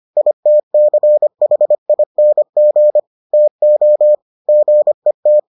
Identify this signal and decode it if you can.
Morse code, heard as a keyed tone, about 25 wpm, 600 Hz. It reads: ITCHING TO GET